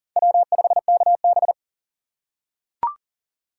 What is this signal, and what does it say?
Morse code, heard as a keyed tone, about 40 wpm, 700 Hz. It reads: W5KB
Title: USA Call Signs - 4 Characters - 4x Character Spacing - 40wpm